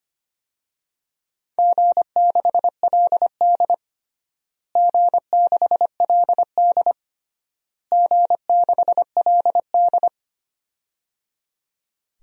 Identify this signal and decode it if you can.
Morse code, heard as a keyed tone, about 25 wpm, 700 Hz. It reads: G6LD G6LD G6LD
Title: DXCC-100 Call Signs - 4 Characters - Repeated 3x - 25wpm